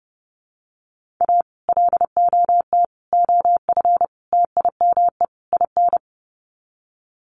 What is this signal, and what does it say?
Morse code, heard as a keyed tone, about 30 wpm, 700 Hz. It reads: A LOT OF TIME IN
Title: Sets of 5 Words Encore2 30wpm